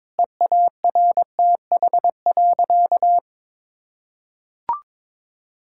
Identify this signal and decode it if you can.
Morse code, heard as a keyed tone, about 21 wpm, 700 Hz. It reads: EARTH.
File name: Difficult Sentences with Hints - 22wpm